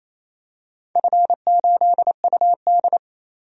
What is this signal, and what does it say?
Morse code, heard as a keyed tone, about 28 wpm, 700 Hz. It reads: F8UD
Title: DXCC-100 Call Signs - 4 Characters - 28wpm